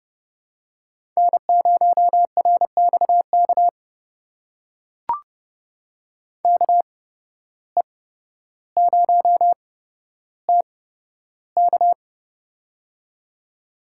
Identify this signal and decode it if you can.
Morse code, heard as a keyed tone, about 30 wpm, spaced as for 13 wpm, 700 Hz. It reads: N0RXK KE0TK
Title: USA Call Signs - 8x Character Spacing - 30wpm